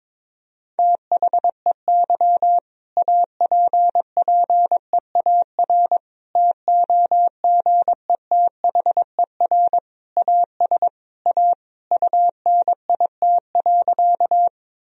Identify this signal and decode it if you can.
Morse code, heard as a keyed tone, about 22 wpm, 700 Hz. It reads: THEY APPEAR TOGETHER AS A UNIT.
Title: Sentences from Top 1000 Words - Review Part 2 - 22wpm